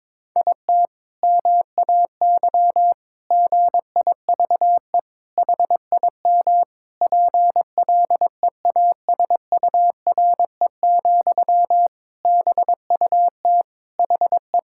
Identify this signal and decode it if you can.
Morse code, heard as a keyed tone, about 22 wpm, 700 Hz. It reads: IT MAY GIVE HIM PLEASURE, BUT HE
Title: Sentences from Top 1600 Words 22wpm